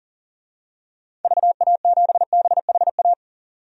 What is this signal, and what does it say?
Morse code, heard as a keyed tone, about 40 wpm, 700 Hz. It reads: VA7BHA
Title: DXCC-100 Call Signs - 6 Characters - 40wpm